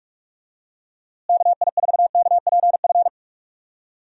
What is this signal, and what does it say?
Morse code, heard as a keyed tone, about 46 wpm, 700 Hz. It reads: KI4KPF
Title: USA Call Signs - 6 Characters - Repeated 3x - 45wpm